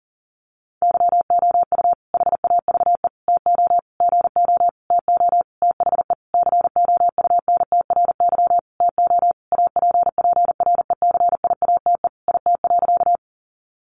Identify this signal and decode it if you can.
Morse code, heard as a keyed tone, about 40 wpm, 700 Hz. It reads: YOU HAVE TO GO TO THE COUNTRY TO APPRECIATE IT.
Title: Sentences from Top 1300 Words 40wpm